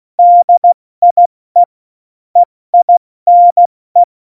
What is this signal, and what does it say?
Morse code, heard as a keyed tone, about 16 wpm, 700 Hz. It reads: DIE EINE
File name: Morsekurs / Witze / Witz 18 August 2020, 70 F80